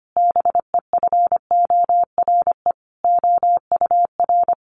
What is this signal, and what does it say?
Morse code, heard as a keyed tone, about 25 wpm, 700 Hz. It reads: BEFORE OUR
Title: A Sound Of Thunder 25wpm